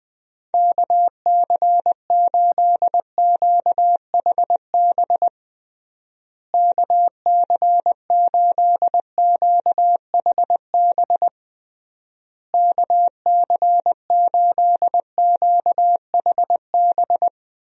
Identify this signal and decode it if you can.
Morse code, heard as a keyed tone, about 20 wpm, 700 Hz. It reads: KC8QHB KC8QHB KC8QHB
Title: USA Call Signs - 6 Characters - Repeated 3x - 20wpm